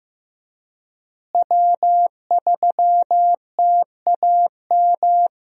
Morse code, heard as a keyed tone, about 15 wpm, 700 Hz. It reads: W3TAM
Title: USA Call Signs - 5 Characters - Repeated 3x - 15wpm